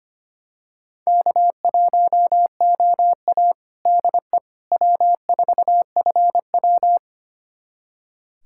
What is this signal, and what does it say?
Morse code, heard as a keyed tone, about 25 wpm, 700 Hz. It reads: K1OA DE W4FW
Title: USA Callsign DE Callsign - 4 Characters - 25wpm